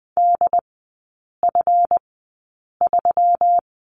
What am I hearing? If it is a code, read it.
Morse code, heard as a keyed tone, about 20 wpm, spaced as for 8 wpm, 700 Hz. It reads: DF3
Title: 205 Lesson 33e - Letter j - Three Characters 20wpm